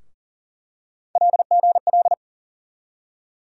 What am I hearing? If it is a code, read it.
Morse code, heard as a keyed tone, about 40 wpm, 700 Hz. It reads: LGL
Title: USA Call Sign Postfixes - 3 Letters 40wpm